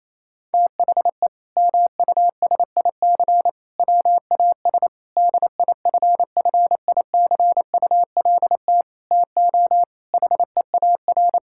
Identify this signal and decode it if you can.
Morse code, heard as a keyed tone, about 28 wpm, 700 Hz. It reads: THE MUSIC WAS DIFFICULT TO HEAR
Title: Sentences from Top 1000 Words - Review Part 6 - 28wpm